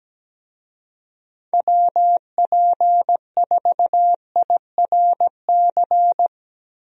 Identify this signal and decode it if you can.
Morse code, heard as a keyed tone, about 17 wpm, 700 Hz. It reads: WP4IRC
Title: USA Call Signs - 6 Characters - Repeated 3x - 17wpm